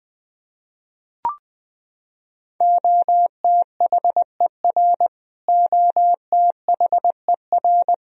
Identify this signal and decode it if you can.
Morse code, heard as a keyed tone, about 20 wpm, 700 Hz. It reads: OTHER OTHER
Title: Top 100 Words Repeated 20wpm